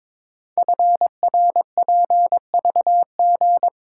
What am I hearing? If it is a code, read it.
Morse code, heard as a keyed tone, about 22 wpm, 700 Hz. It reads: FRPVG
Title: Code Groups - 5 Letters - 22wpm